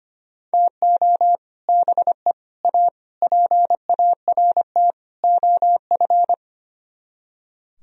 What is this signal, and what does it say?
Morse code, heard as a keyed tone, about 33 wpm, 700 Hz. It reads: TO BE A PART OF